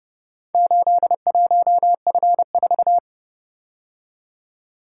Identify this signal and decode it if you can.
Morse code, heard as a keyed tone, about 29 wpm, 700 Hz. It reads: 81F4